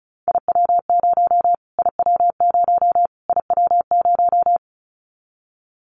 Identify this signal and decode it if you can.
Morse code, heard as a keyed tone, about 34 wpm, 700 Hz. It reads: IW0 IW0 IW0
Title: Callsign Prefixes for Top 25 DXCC - Repeated 3x - 35wpm